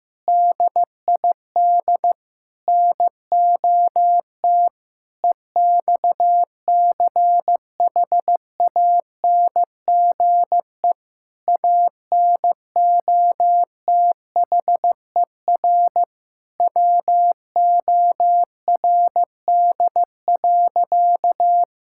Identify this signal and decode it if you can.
Morse code, heard as a keyed tone, about 15 wpm, 700 Hz. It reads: DID NOT EXCHANGE ANOTHER WORD.
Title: Sentences from Top 1000 Words - Review Part 6 - 15wpm